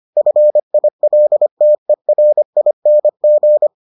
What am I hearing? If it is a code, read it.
Morse code, heard as a keyed tone, about 25 wpm, 600 Hz. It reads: FILTERING